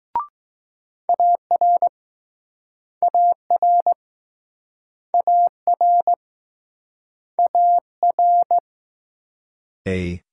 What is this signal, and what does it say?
Morse code, heard as a keyed tone, about 21 wpm, 700 Hz. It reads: AR AR AR AR